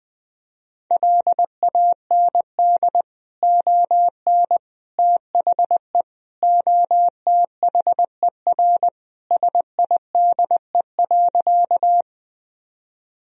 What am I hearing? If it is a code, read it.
Morse code, heard as a keyed tone, about 20 wpm, 700 Hz. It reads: LAND ON THE OTHER SIDE.
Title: Sentences from Top 1000 Words - Review Part 1 - 20wpm